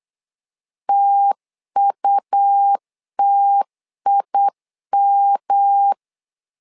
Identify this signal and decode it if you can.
Morse code, heard as a keyed tone, about 8 wpm, 800 Hz. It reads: TUTIM
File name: LEZIONI / 3 MWRO / 3 M 8wpm 2300